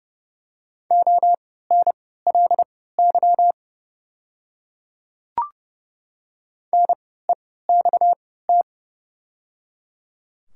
Morse code, heard as a keyed tone, about 31 wpm, spaced as for 17 wpm, 700 Hz. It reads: ONLY NEXT
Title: 3x Character Spacing - Top 300 Words - 30wpm